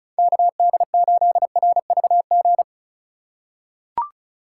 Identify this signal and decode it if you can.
Morse code, heard as a keyed tone, about 36 wpm, 700 Hz. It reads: KD8RVG
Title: USA Call Signs - 6 Characters - 4x Character Spacing - 35wpm